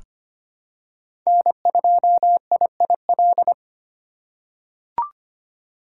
Morse code, heard as a keyed tone, about 25 wpm, 700 Hz. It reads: N2IIL
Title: USA Call Signs - 5 Characters - 2x Character Spacing - 25wpm